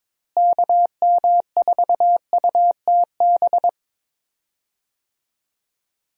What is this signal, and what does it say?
Morse code, heard as a keyed tone, about 22 wpm, 700 Hz. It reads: KM4UTB
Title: USA Call Signs - 6 Characters - Repeated 2x - 22wpm